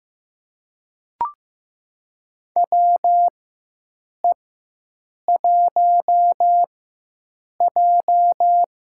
Morse code, heard as a keyed tone, about 15 wpm, spaced as for 7 wpm, 700 Hz. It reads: WE1J
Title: USA Call Signs - 4 Characters - 4x Character Spacing - 15wpm